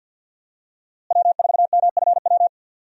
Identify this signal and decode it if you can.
Morse code, heard as a keyed tone, about 53 wpm, 700 Hz. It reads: W4MFW
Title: USA Call Signs - 5 Characters - Repeated 2x - 50wpm